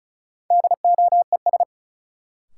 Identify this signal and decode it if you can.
Morse code, heard as a keyed tone, about 36 wpm, 700 Hz. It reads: DOES
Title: Sets of 2 Words - New Word Drill - 35wpm